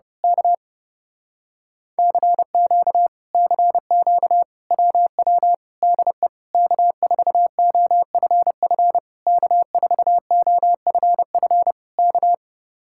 Morse code, heard as a keyed tone, about 28 wpm, 700 Hz. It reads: K CQ CQ WW DE K4OFF K4OFF K
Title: Speed-Racing Common QSO Phrases - 30wpm